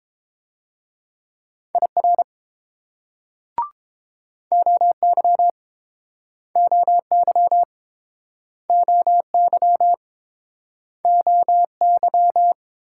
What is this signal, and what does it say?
Morse code, heard as a keyed tone, about 28 wpm, 700 Hz. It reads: IR OY OY OY OY